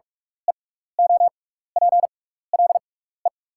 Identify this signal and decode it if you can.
Morse code, heard as a keyed tone, about 44 wpm, spaced as for 15 wpm, 700 Hz. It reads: EOPLE
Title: 6x Character Spacing - Top 100 Words - 45wpm